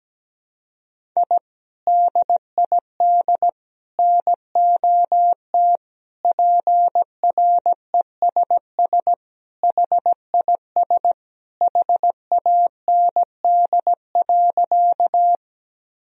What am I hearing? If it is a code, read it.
Morse code, heard as a keyed tone, about 17 wpm, 700 Hz. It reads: I DID NOT PRESS HIS HAND.
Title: Sentences from Top 1000 Words - Review Part 1 - 17wpm